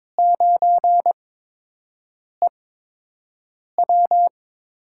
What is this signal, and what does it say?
Morse code, heard as a keyed tone, about 22 wpm, spaced as for 6 wpm, 700 Hz. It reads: 9EW
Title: USA Call Signs - 4 Characters - 8x Character Spacing - 22wpm